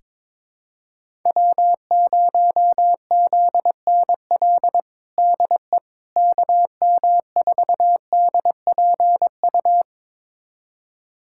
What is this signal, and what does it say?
Morse code, heard as a keyed tone, about 22 wpm, 700 Hz. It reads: W0ZNL DE KM4DPU
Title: USA Callsign DE Callsign 22wpm